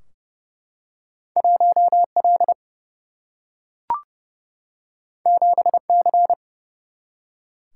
Morse code, heard as a keyed tone, about 30 wpm, 700 Hz. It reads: 1 L 7 C
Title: Single-Letter Single-Number 30wpm